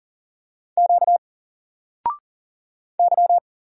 Morse code, heard as a keyed tone, about 40 wpm, 700 Hz. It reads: Q Y
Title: Single Letter-Number Alt 40wpm